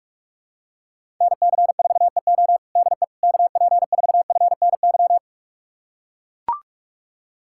Morse code, heard as a keyed tone, about 42 wpm, 700 Hz. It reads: NC4EO DE KP4FNY
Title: USA Callsign DE Callsign 45wpm